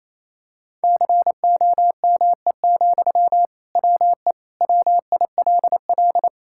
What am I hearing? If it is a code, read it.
Morse code, heard as a keyed tone, about 28 wpm, 700 Hz. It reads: COME, WE WILL
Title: Sentences from Top 1600 Words 28wpm